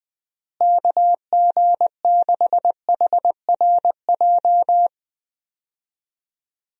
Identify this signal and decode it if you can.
Morse code, heard as a keyed tone, about 21 wpm, 700 Hz. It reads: KG6HRJ
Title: USA Call Signs - 6 Characters - Repeated 3x - 20wpm